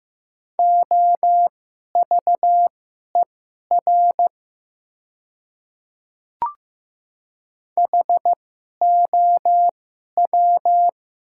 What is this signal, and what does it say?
Morse code, heard as a keyed tone, about 15 wpm, spaced as for 11 wpm, 690 Hz. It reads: OVER HOW
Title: 2x Character Spacing - Top 100 Words - 15wpm